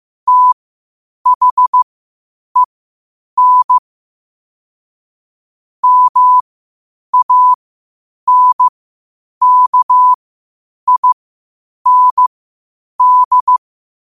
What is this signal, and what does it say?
Morse code, heard as a keyed tone, about 15 wpm, spaced as for 8 wpm, 1000 Hz. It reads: THEN MANKIND
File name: qotdfiles / 8wpm